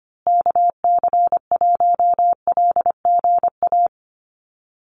Morse code, heard as a keyed tone, about 25 wpm, 700 Hz. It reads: KC1LGA
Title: USA Call Signs Repeated 3x 25wpm